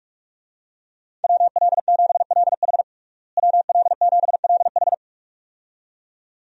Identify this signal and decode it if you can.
Morse code, heard as a keyed tone, about 42 wpm, 700 Hz. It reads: WL7LH WL7LH
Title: USA Call Signs - 5 Characters - Repeated 2x - 45wpm